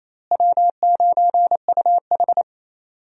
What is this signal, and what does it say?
Morse code, heard as a keyed tone, about 28 wpm, 700 Hz. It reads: W9UH
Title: USA Call Signs - 4 Characters - Repeated 2x - 28wpm